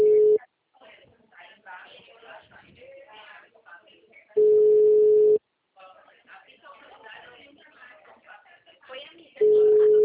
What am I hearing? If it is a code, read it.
Ringback tone (Europe, 425 Hz), with speech.